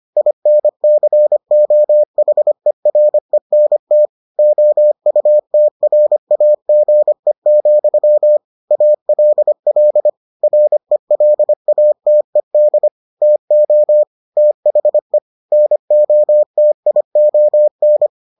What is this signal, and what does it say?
Morse code, heard as a keyed tone, about 25 wpm, 600 Hz. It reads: INCOHERENT OUTRAGE, ALL RELATED TO THE NOTION